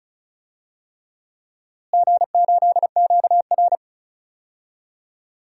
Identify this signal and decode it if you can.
Morse code, heard as a keyed tone, about 34 wpm, 700 Hz. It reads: G8QR
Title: DXCC-100 Call Signs - 4 Characters - Repeated 3x - 35wpm